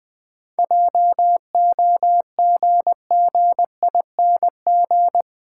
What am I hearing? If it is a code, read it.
Morse code, heard as a keyed tone, about 21 wpm, 700 Hz. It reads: JOGGING
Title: Difficult Sentences with Hints - 20wpm